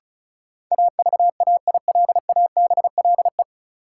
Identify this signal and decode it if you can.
Morse code, heard as a keyed tone, about 36 wpm, 700 Hz. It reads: AVAILABLE
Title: Sentences from Top 600 Words - New Word Drill - 35wpm